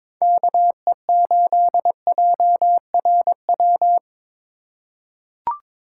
Morse code, heard as a keyed tone, about 22 wpm, 700 Hz. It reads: KE8JRW
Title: USA Call Signs - 6 Characters - 2x Character Spacing - 22wpm